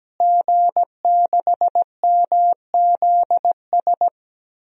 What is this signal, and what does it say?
Morse code, heard as a keyed tone, about 17 wpm, 700 Hz. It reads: G6MZS
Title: DXCC-100 Call Signs - 5 Characters - Repeated 2x - 17wpm